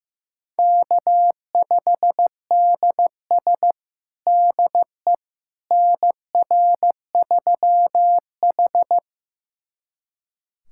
Morse code, heard as a keyed tone, about 15 wpm, 700 Hz. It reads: K5DS DE NR3H
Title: USA Callsign DE Callsign - 4 Characters - 15wpm